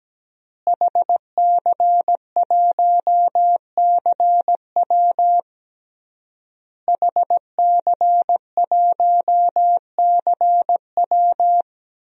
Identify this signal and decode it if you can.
Morse code, heard as a keyed tone, about 17 wpm, 700 Hz. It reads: HC1CW HC1CW
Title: DXCC-100 Call Signs - 5 Characters - Repeated 3x - 17wpm